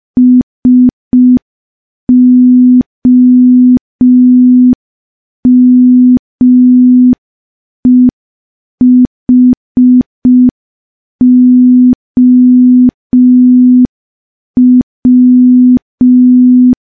Morse code, heard as a keyed tone, about 5 wpm, 250 Hz. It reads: SOMEHOW
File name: morse / morse-training-05WPM